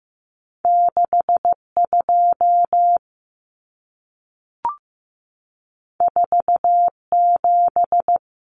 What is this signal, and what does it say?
Morse code, heard as a keyed tone, about 15 wpm, 700 Hz. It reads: 62 47
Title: Numbers - 2-Digit 15wpm